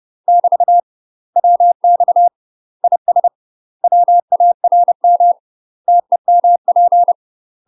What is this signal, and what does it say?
Morse code, heard as a keyed tone, about 30 wpm, 700 Hz. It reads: <BT> WX IS WARM TEMP
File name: UK MASTER QSOs TRAINING/30WPM